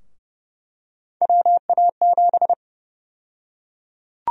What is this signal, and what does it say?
Morse code, heard as a keyed tone, about 30 wpm, 700 Hz. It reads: WA7
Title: USA Call Sign Prefixes - 30wpm